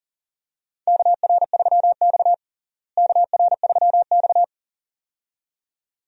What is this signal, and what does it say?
Morse code, heard as a keyed tone, about 37 wpm, 700 Hz. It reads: KR3X KR3X
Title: USA Call Signs - 4 Characters - Repeated 2x - 40wpm